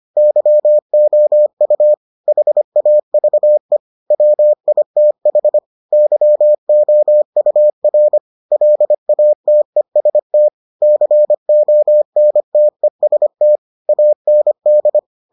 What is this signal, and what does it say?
Morse code, heard as a keyed tone, about 25 wpm, 600 Hz. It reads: YOU HAVE WITH YOUR LATEST CONTEST AND